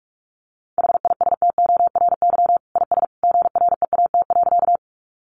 Morse code, heard as a keyed tone, about 44 wpm, 700 Hz. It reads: HISTORY IS GREAT.
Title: Sentences from Top 600 Words 45wpm